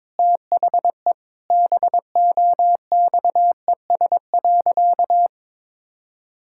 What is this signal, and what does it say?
Morse code, heard as a keyed tone, about 22 wpm, 700 Hz. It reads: THE BOXES.
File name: practice / 22wpm / Sentences with 3-Word Phrases - 22wpm